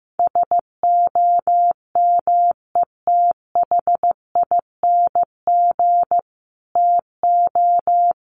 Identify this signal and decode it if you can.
Morse code, heard as a keyed tone, about 15 wpm, 700 Hz. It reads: SOMETHING TO
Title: Sentences from Top 1500 Words 15wpm